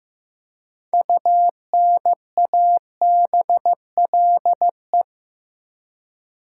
Morse code, heard as a keyed tone, about 15 wpm, 700 Hz. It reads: UNABLE
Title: Six Letter Words - 15wpm